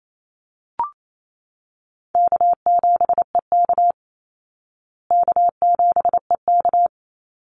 Morse code, heard as a keyed tone, about 28 wpm, 700 Hz. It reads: K7EK K7EK
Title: Easy Call Signs Repeated 3x - 28wpm